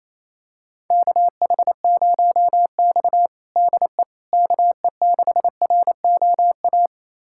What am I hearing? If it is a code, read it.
Morse code, heard as a keyed tone, about 28 wpm, 700 Hz. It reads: KH0X DE KE6ROA
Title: USA Callsign DE Callsign 28wpm